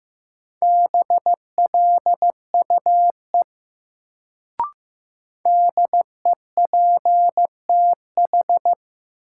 Morse code, heard as a keyed tone, about 15 wpm, 700 Hz. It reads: BLUE DEPTH
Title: Sentences from Top 1200 Words - New Word Drill Alt - 15wpm